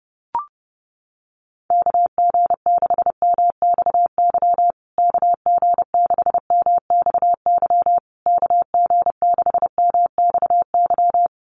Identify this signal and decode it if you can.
Morse code, heard as a keyed tone, about 30 wpm, 700 Hz. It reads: KG6MXY KG6MXY KG6MXY
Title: USA Call Signs Repeated 3x 30wpm